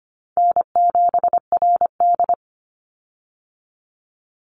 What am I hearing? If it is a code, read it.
Morse code, heard as a keyed tone, about 25 wpm, 700 Hz. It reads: N7RD
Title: Easy Call Signs - 25wpm